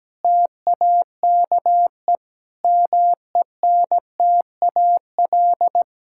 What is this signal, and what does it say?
Morse code, heard as a keyed tone, about 17 wpm, 700 Hz. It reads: TAKE MENTAL
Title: Sentences from Top 1000 Words - Review Part 6 - 17wpm